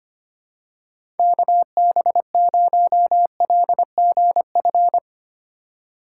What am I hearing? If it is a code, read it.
Morse code, heard as a keyed tone, about 25 wpm, 700 Hz. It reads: KB0LGF
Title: K1USN SST QSO Practice - 25wpm